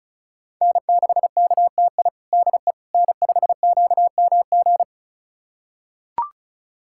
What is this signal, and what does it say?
Morse code, heard as a keyed tone, about 36 wpm, 700 Hz. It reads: N6KTI DE N5QMG
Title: USA Callsign DE Callsign - 5 Characters - 35wpm